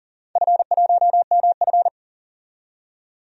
Morse code, heard as a keyed tone, about 40 wpm, 700 Hz. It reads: F1MF
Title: DXCC-100 Call Signs - 4 Characters - 40wpm